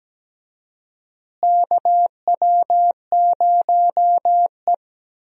Morse code, heard as a keyed tone, about 17 wpm, 700 Hz. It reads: KW0E
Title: USA Call Signs - 4 Characters - Repeated 3x - 17wpm